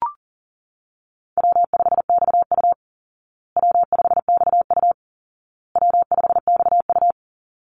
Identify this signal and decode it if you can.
Morse code, heard as a keyed tone, about 40 wpm, 700 Hz. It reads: W5XU W5XU W5XU
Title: Easy Call Signs Repeated 3x - 40wpm